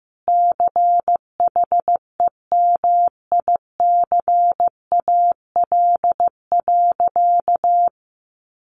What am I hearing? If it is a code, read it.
Morse code, heard as a keyed tone, about 14 wpm, 700 Hz. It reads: CHEMICAL.
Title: Sentences from Top 1200 Words 15wpm